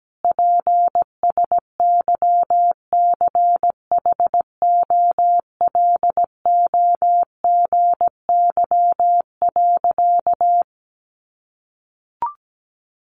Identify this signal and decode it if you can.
Morse code, heard as a keyed tone, about 17 wpm, 700 Hz. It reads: PSYCHOLOGY.
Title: Sentences from Top 1400 Words 17wpm